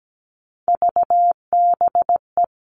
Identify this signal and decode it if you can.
Morse code, heard as a keyed tone, about 17 wpm, 700 Hz. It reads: VBE
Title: All Cut Numbers - 3-Digit 17wpm